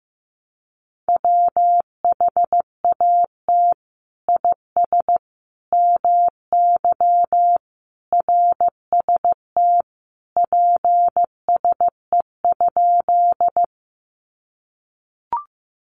Morse code, heard as a keyed tone, about 15 wpm, 700 Hz. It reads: WHAT IS MY RST PSE?
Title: Common QSO Phrases 15wpm